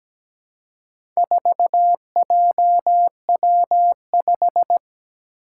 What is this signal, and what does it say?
Morse code, heard as a keyed tone, about 17 wpm, 700 Hz. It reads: 4JW5
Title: Four Challenging Letter-Number Combinations 17wpm